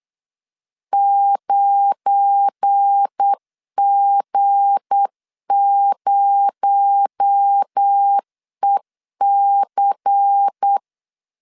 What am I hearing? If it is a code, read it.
Morse code, heard as a keyed tone, about 8 wpm, 800 Hz. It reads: 9G0EC
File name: LEZIONI / 7 GF9Z / 7 GF9 8wpm 3545